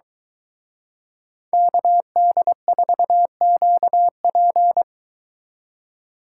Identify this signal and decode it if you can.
Morse code, heard as a keyed tone, about 23 wpm, 700 Hz. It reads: KD4QP